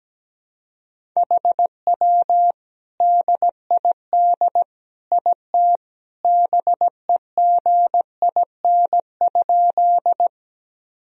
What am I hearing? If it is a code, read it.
Morse code, heard as a keyed tone, about 17 wpm, 700 Hz. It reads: HW DID IT BEGIN?
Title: Sentences from Top 300 Words and Abbreviations 17wpm